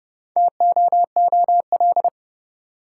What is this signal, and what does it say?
Morse code, heard as a keyed tone, about 30 wpm, 700 Hz. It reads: TOOL
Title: Sentences from Top 1000 Words - Review Part 6 - New Word Drill - 30wpm